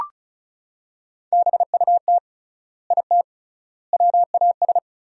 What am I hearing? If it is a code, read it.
Morse code, heard as a keyed tone, about 36 wpm, 700 Hz. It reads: BUT IT WAS
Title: 3x Word-Spacing - Sentences from Top 300 Words - 35wpm